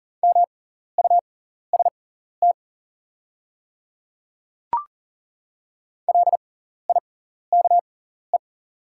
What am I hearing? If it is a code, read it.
Morse code, heard as a keyed tone, about 40 wpm, spaced as for 14 wpm, 690 Hz. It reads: MUST LIKE
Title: 6x Character Spacing - Top 500 Words - 40wpm